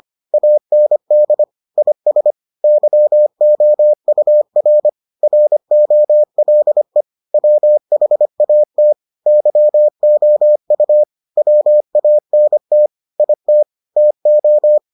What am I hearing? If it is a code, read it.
Morse code, heard as a keyed tone, about 25 wpm, 600 Hz. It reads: AND IS YOUR ROLE WHAT YOU WANT IT TO